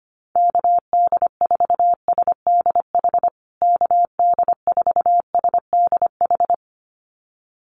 Morse code, heard as a keyed tone, about 25 wpm, 700 Hz. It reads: KD4SDH KD4SDH
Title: USA Call Signs Repeated 2x 25wpm